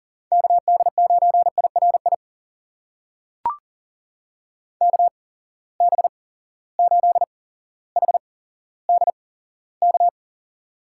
Morse code, heard as a keyed tone, about 40 wpm, spaced as for 17 wpm, 700 Hz. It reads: KD9IRI KB8HDK
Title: USA Call Signs - 6 Characters - 8x Character Spacing - 40wpm